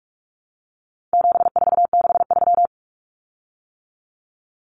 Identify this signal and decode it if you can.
Morse code, heard as a keyed tone, about 44 wpm, 700 Hz. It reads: 7463
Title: Common Cut Numbers - 4-Digit 45wpm